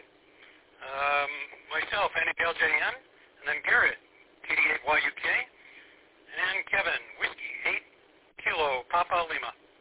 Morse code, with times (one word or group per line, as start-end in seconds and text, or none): none